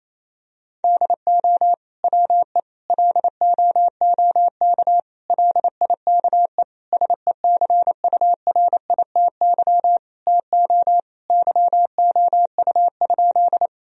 Morse code, heard as a keyed tone, about 28 wpm, 700 Hz. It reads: DO WE LOOK LIKE SECURITY TO YOU?
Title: Sentences from Top 1000 Words - Review Part 4 - 28wpm